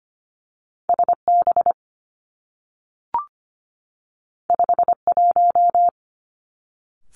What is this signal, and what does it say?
Morse code, heard as a keyed tone, about 25 wpm, 700 Hz. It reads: S B 5 1
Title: Two Challenging Letter-Number Combinations 25wpm